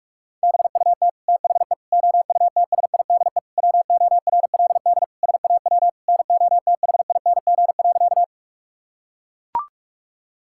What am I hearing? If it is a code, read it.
Morse code, heard as a keyed tone, about 41 wpm, 700 Hz. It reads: BUT THE OUTSIDE WORLD SAW NOTHING.
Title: Sentences from Top 1000 Words - Review Part 6 - 45wpm